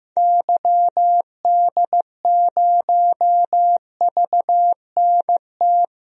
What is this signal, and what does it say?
Morse code, heard as a keyed tone, about 15 wpm, 700 Hz. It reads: YD0VNT
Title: DXCC-100 Call Signs - 6 Characters - 15wpm